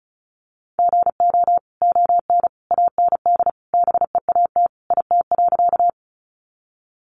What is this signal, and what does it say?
Morse code, heard as a keyed tone, about 34 wpm, 700 Hz. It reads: GO ON AND BEAT IT.
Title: Sentences from Top 1300 Words 35wpm